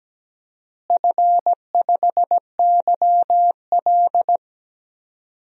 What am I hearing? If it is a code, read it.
Morse code, heard as a keyed tone, about 17 wpm, 700 Hz. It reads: F5YL
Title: DXCC-100 Call Signs - 4 Characters - Repeated 2x - 17wpm